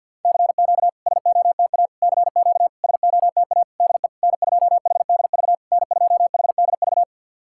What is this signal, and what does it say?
Morse code, heard as a keyed tone, about 50 wpm, 700 Hz. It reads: CQ SOTA CQ SOTA DE N2HDV N2HDV
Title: SOTA QSO Practice 50wpm